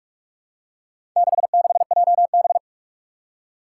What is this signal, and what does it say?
Morse code, heard as a keyed tone, about 46 wpm, 700 Hz. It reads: 66JB
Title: Four Challenging Letter-Number Combinations 45wpm